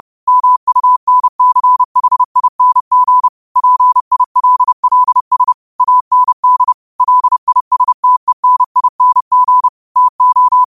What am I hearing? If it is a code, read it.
Morse code, heard as a keyed tone, about 30 wpm, 1000 Hz. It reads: MUNCHING PILLS AND LISTENING TO
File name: qotdfiles / 30wpm